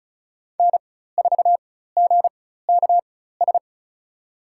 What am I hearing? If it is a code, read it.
Morse code, heard as a keyed tone, about 36 wpm, spaced as for 16 wpm, 700 Hz. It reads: N4GKS
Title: USA Call Signs - 5 Characters - 4x Character Spacing - 35wpm